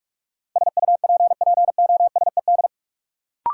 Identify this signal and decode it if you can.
Morse code, heard as a keyed tone, about 43 wpm, 700 Hz. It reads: SUPPOSED
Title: Sets of 3 Words - New Word Drill - 45wpm